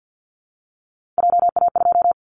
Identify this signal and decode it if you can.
Morse code, heard as a keyed tone, about 50 wpm, 700 Hz. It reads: JA2
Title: Callsign Prefixes for Top 25 DXCC - 50wpm